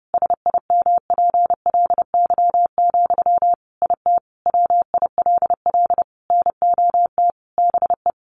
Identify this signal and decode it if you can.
Morse code, heard as a keyed tone, about 30 wpm, 700 Hz. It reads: SIMPLY, IT WILL NOT BE
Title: Sentences from Top 800 Words 30wpm